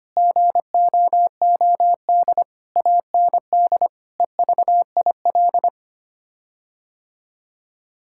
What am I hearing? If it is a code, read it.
Morse code, heard as a keyed tone, about 25 wpm, 700 Hz. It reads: GOOD AND EVIL
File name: Sets of 9 Words - 25wpm